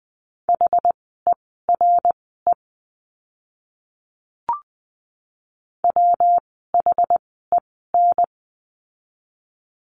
Morse code, heard as a keyed tone, about 20 wpm, spaced as for 15 wpm, 700 Hz. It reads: HERE WHEN
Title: 244 Lesson 39f - Letter z - Words 20wpm